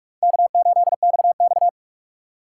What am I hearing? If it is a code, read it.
Morse code, heard as a keyed tone, about 46 wpm, 700 Hz. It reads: K8XX
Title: USA Call Signs - 4 Characters - 4x Character Spacing - 45wpm